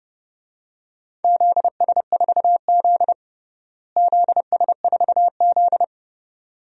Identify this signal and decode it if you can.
Morse code, heard as a keyed tone, about 30 wpm, 700 Hz. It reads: ZS4Z ZS4Z
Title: DXCC-100 Call Signs - 4 Characters - Repeated 2x - 30wpm